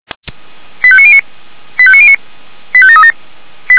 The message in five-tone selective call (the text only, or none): 85909 85909 85737